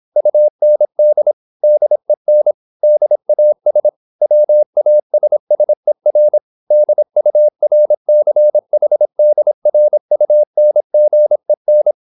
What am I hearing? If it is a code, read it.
Morse code, heard as a keyed tone, about 26 wpm, 600 Hz. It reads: UND DEN DAS WASSER DURCHDRUNGEN